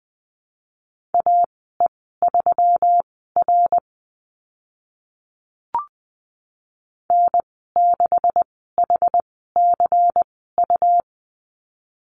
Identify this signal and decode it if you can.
Morse code, heard as a keyed tone, about 20 wpm, spaced as for 15 wpm, 700 Hz. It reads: AE3R N6HCU
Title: 193 Lesson 31g - Letter v - Call Signs 20wpm